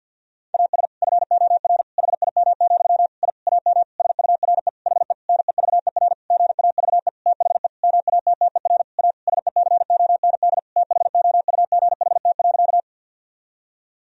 Morse code, heard as a keyed tone, about 52 wpm, 700 Hz. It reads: AS FOR HIM, I AM SURE HE NEVER GAVE THE MATTER A SECOND THOUGHT.
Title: Sentences from Top 1000 Words - Review Part 2 - 50wpm